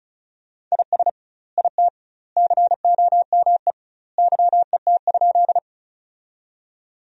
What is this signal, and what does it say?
Morse code, heard as a keyed tone, about 36 wpm, 700 Hz. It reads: IS IT COME YET?
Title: 2x Word-Spacing - Sentences from Top 500 Words - 35wpm